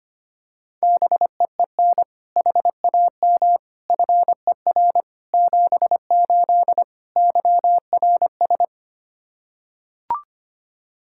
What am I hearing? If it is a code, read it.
Morse code, heard as a keyed tone, about 25 wpm, 700 Hz. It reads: BEEN HAM FER 78 YRS
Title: K1USN SST QSO Practice 25wpm